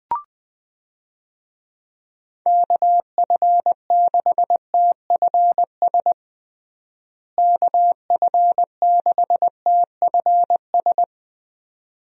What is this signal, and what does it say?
Morse code, heard as a keyed tone, about 21 wpm, 700 Hz. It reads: KF6TFS KF6TFS
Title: USA Call Signs - 6 Characters - Repeated 2x - 20wpm